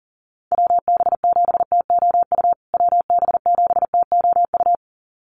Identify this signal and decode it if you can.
Morse code, heard as a keyed tone, about 40 wpm, 700 Hz. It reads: WB7TOU WB7TOU
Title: USA Call Signs Repeated 2x 40wpm